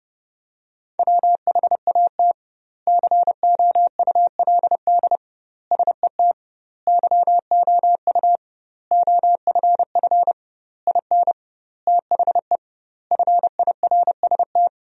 Morse code, heard as a keyed tone, about 30 wpm, 700 Hz. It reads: WHAT COULD SET YOU OFF IN THE FIRST
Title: 2x Word-Spacing - Sentences from Top 200 Words - 30wpm